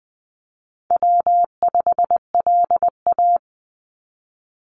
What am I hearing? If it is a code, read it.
Morse code, heard as a keyed tone, about 20 wpm, 700 Hz. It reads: W5LA
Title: Call Signs Repeated 3x - 20wpm